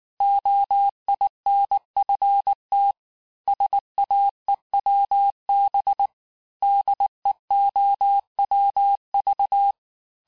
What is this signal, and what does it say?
Morse code, heard as a keyed tone, about 19 wpm, 800 Hz. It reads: OINFT SAEWB DEOWV
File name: cours-du-lundi files / cours-1589-texte-chiffre-vitesse-18mpm